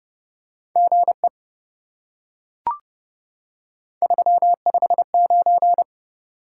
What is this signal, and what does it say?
Morse code, heard as a keyed tone, about 28 wpm, 700 Hz. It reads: GE 359